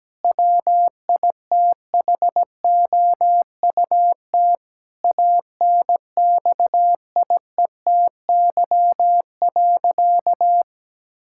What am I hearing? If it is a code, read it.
Morse code, heard as a keyed tone, about 17 wpm, 700 Hz. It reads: WITHOUT ANXIETY.
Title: Sentences from Top 1600 Words 17wpm